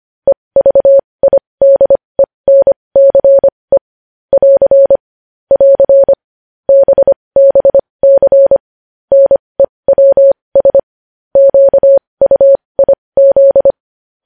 Morse code, heard as a keyed tone, about 25 wpm, spaced as for 20 wpm, 550 Hz. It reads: EVIDENCE <AR> <AR> BBC NEWS QUIZ
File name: podcast / News headlines-1x-25x20